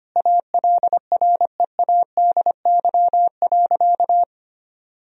Morse code, heard as a keyed tone, about 25 wpm, 700 Hz. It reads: ALREADY.
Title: Sentences from Top 1000 Words - Review Part 3 - 25wpm